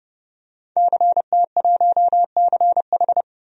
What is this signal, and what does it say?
Morse code, heard as a keyed tone, about 30 wpm, 700 Hz. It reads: CT1CH
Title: DXCC-100 Call Signs - 5 Characters - 30wpm